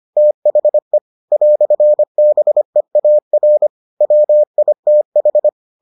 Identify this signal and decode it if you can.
Morse code, heard as a keyed tone, about 25 wpm, 600 Hz. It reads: THE "BEAR WITH